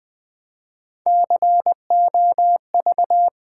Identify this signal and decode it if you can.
Morse code, heard as a keyed tone, about 20 wpm, 700 Hz. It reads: COV
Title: Build A Word - 3 Letters At A Time - 8-Letter Words - 20wpm